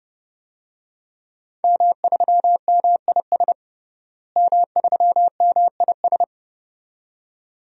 Morse code, heard as a keyed tone, about 32 wpm, 700 Hz. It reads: M3MIS M3MIS
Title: DXCC-100 Call Signs - 5 Characters - Repeated 2x - 30wpm